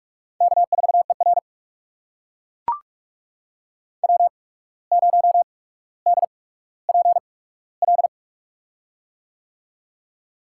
Morse code, heard as a keyed tone, about 44 wpm, spaced as for 18 wpm, 700 Hz. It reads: K4ER W0DPL
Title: USA Call Signs - 8x Character Spacing - 45wpm